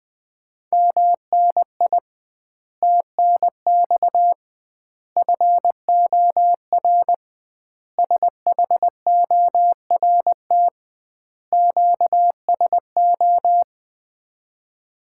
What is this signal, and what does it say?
Morse code, heard as a keyed tone, about 21 wpm, 700 Hz. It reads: MNI TNX FOR SHORT QSO
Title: Common QSO Phrases - 2x Word-Spacing - 20wpm